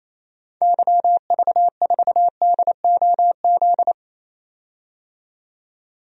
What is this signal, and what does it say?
Morse code, heard as a keyed tone, about 28 wpm, 700 Hz. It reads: YV4DOZ
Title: DXCC-100 Call Signs - 6 Characters - 28wpm